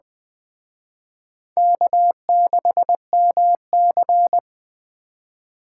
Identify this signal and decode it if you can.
Morse code, heard as a keyed tone, about 20 wpm, 700 Hz. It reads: K6MC
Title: Speed Racing - USA Call Signs 17wpm